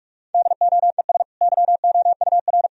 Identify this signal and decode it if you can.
Morse code, heard as a keyed tone, about 42 wpm, 700 Hz. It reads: DOES YOUR
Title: Sentences from Top 1000 Words - Review Part 3 - 45wpm